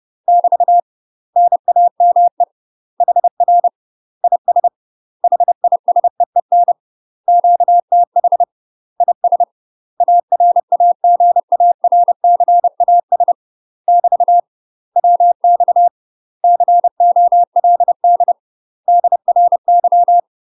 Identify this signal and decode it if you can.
Morse code, heard as a keyed tone, about 30 wpm, 700 Hz. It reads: <BT> NAME HR IS HISEEN QTH IS ARAGARCAS <BT> WX COLD DRY
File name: UK MASTER QSOs TRAINING / 30WPM / UK MASTER QSOs 0009